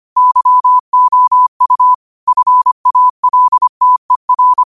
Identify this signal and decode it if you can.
Morse code, heard as a keyed tone, about 25 wpm, 1000 Hz. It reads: YOU FALTER